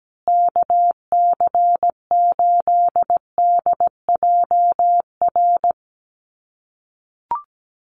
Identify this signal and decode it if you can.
Morse code, heard as a keyed tone, about 17 wpm, 700 Hz. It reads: KC8DJR
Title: USA Call Signs Repeated 2x 17wpm